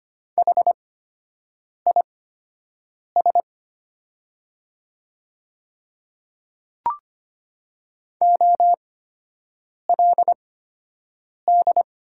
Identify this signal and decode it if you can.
Morse code, heard as a keyed tone, about 25 wpm, spaced as for 7 wpm, 700 Hz. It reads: HIS OLD
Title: 8x Character Spacing - Top 200 Words - 25wpm